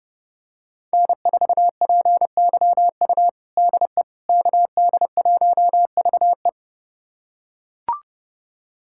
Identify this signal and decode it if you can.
Morse code, heard as a keyed tone, about 30 wpm, 700 Hz. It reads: N4PYU DE KD1VE
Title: USA Callsign DE Callsign - 5 Characters - 30wpm